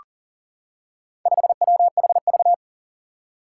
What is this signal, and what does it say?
Morse code, heard as a keyed tone, about 40 wpm, 700 Hz. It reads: 5WHV
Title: Four Challenging Letter-Number Combinations 40wpm